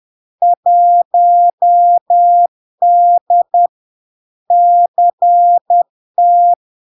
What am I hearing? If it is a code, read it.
Morse code, heard as a keyed tone, about 10 wpm, 700 Hz. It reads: 1D CT
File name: LARK-FD-2022-Log-in-CW-10wpm-700-Hz